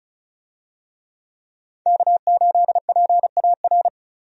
Morse code, heard as a keyed tone, about 36 wpm, 700 Hz. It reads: K8PAR
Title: USA Call Signs - 5 Characters - 35wpm